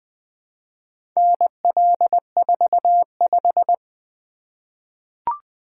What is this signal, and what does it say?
Morse code, heard as a keyed tone, about 20 wpm, 700 Hz. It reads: NL45
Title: Four Letter-Number Combinations - 20wpm